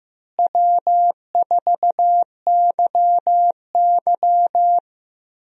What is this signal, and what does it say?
Morse code, heard as a keyed tone, about 15 wpm, 700 Hz. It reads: W4YY
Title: USA Call Signs - 4 Characters - Repeated 3x - 15wpm